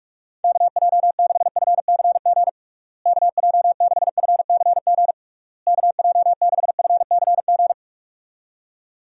Morse code, heard as a keyed tone, about 43 wpm, 700 Hz. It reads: KJ6FCG KJ6FCG KJ6FCG
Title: USA Call Signs - 6 Characters - Repeated 3x - 45wpm